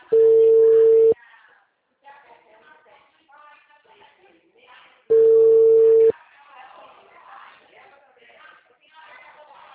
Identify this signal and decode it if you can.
Ringback tone (China, 450 Hz), with speech.